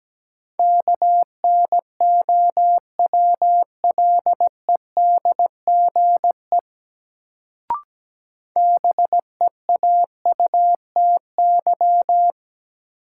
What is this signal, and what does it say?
Morse code, heard as a keyed tone, about 17 wpm, 700 Hz. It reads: KNOWLEDGE BEAUTY
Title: Sentences from Top 600 Words - New Word Drill - 17wpm